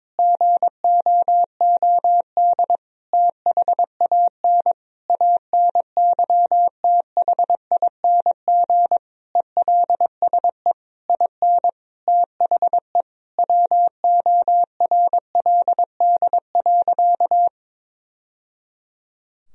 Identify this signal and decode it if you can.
Morse code, heard as a keyed tone, about 22 wpm, 700 Hz. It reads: GOOD THAN ANYTHING ELSE IN THE WORLD.
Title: Sentences from Top 1000 Words - Review Part 4 - 22wpm